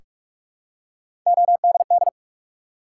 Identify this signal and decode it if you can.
Morse code, heard as a keyed tone, about 44 wpm, 700 Hz. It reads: ODD
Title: Three Letter Words - 45wpm